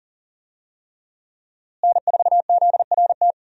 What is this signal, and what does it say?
Morse code, heard as a keyed tone, about 40 wpm, 700 Hz. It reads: N4ZRT
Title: USA Call Signs - 5 Characters - Repeated 3x - 40wpm